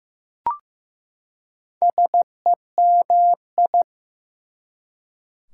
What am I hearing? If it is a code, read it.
Morse code, heard as a keyed tone, about 15 wpm, 700 Hz. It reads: SEMI
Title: Prefixes and Suffixes 15wpm